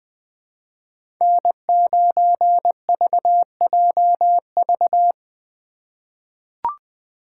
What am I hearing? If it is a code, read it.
Morse code, heard as a keyed tone, about 20 wpm, 700 Hz. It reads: N9VJV
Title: USA Call Signs - 5 Characters - 3x Character Spacing - 20wpm